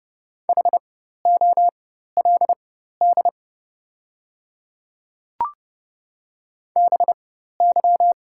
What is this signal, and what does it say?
Morse code, heard as a keyed tone, about 30 wpm, spaced as for 14 wpm, 700 Hz. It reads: HOLD BY
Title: 4x Character Spacing - Top 500 Words - 30wpm